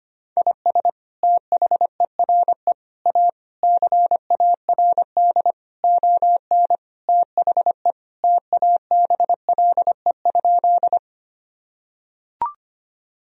Morse code, heard as a keyed tone, about 25 wpm, 700 Hz. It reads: IS THERE A CARD ON THE TABLE?
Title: Sentences from Top 1000 Words - Review Part 4 - 25wpm